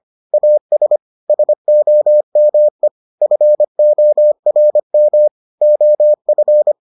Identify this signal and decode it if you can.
Morse code, heard as a keyed tone, about 25 wpm, 600 Hz. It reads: AS SOME FORM OF